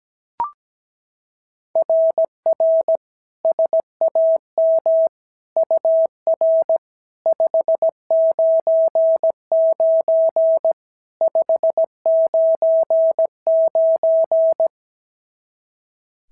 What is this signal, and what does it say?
Morse code, heard as a keyed tone, about 17 wpm, 650 Hz. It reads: RR SAM UR 599 599
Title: SOTA QSO Practice 17wpm